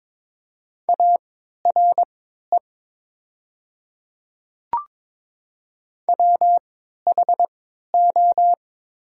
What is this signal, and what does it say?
Morse code, heard as a keyed tone, about 22 wpm, spaced as for 13 wpm, 700 Hz. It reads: ARE WHO
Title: 3x Character Spacing - Top 200 Words - 22wpm